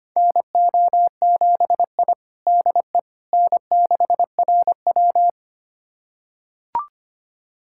Morse code, heard as a keyed tone, about 25 wpm, 700 Hz. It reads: NO7I DE N6RW
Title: USA Callsign DE Callsign - 4 Characters - 25wpm